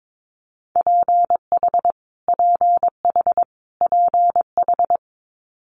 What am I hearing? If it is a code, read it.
Morse code, heard as a keyed tone, about 22 wpm, 700 Hz. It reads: PH PH PH
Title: Multi-letter Phonemes Repeated 3x 22wpm